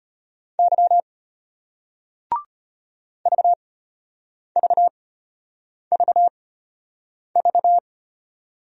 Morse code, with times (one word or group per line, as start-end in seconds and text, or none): 0.6-1.0 Y
3.3-7.8 VVVV